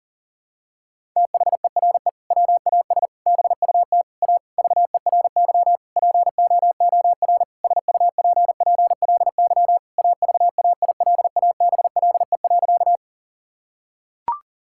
Morse code, heard as a keyed tone, about 40 wpm, 700 Hz. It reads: THERE WAS BUT A VERY POOR SUPPLY AVAILABLE.
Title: Sentences from Top 1000 Words - Review Part 5 - 40wpm